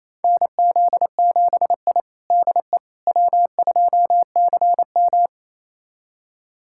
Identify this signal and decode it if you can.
Morse code, heard as a keyed tone, about 27 wpm, 700 Hz. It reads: NZ7I DE W2CM
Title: USA Callsign DE Callsign - 4 Characters - 28wpm